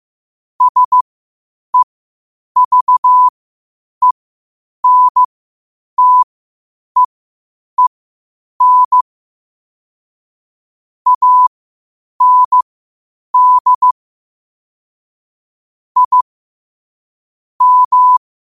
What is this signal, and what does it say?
Morse code, heard as a keyed tone, about 15 wpm, spaced as for 8 wpm, 1000 Hz. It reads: SEVENTEEN AND I M
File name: qotdfiles / 8wpm